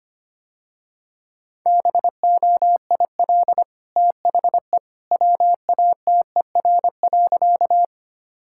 Morse code, heard as a keyed tone, about 25 wpm, 690 Hz. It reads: BOIL THE WATER.
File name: Easy Sentences with Hints - 25wpm